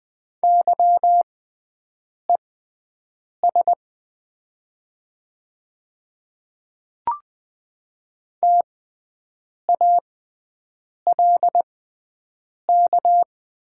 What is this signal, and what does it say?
Morse code, heard as a keyed tone, about 20 wpm, spaced as for 7 wpm, 700 Hz. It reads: YES TALK
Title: 6x Character Spacing - Top 500 Words - 20wpm